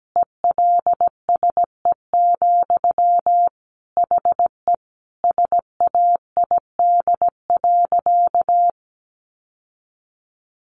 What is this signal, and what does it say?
Morse code, heard as a keyed tone, about 17 wpm, 700 Hz. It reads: ELSE, HE SAID.
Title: Sentences from Top 800 Words 17wpm